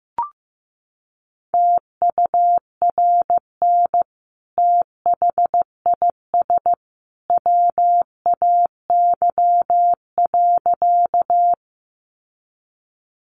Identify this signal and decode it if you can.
Morse code, heard as a keyed tone, about 15 wpm, 700 Hz. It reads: TURN THIS WAY.
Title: Sentences from Top 200 Words 15wpm